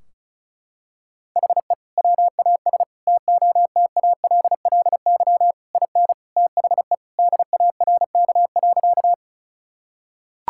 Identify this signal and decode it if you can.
Morse code, heard as a keyed tone, about 36 wpm, 700 Hz. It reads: HE WAS TOTALLY IN THE DARK.
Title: Sentences from Top 1600 Words 35wpm